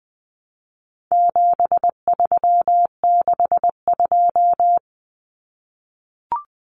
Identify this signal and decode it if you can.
Morse code, heard as a keyed tone, about 20 wpm, 700 Hz. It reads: 7362
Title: Numbers - 4-Digit 20wpm